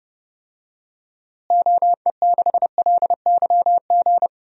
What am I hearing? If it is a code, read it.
Morse code, heard as a keyed tone, about 31 wpm, 700 Hz. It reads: OE6LYG
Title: DXCC-100 Call Signs - 6 Characters - Repeated 2x - 30wpm